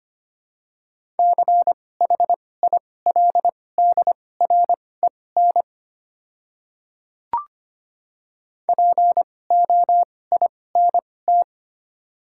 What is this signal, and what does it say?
Morse code, heard as a keyed tone, about 25 wpm, spaced as for 18 wpm, 700 Hz. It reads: CHILDREN POINT
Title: 2x Character Spacing - Top 500 Words - 25wpm